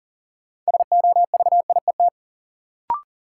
Morse code, heard as a keyed tone, about 38 wpm, 700 Hz. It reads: SOVIET
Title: Sets of 5 Words Encore - New Word Drill - 40wpm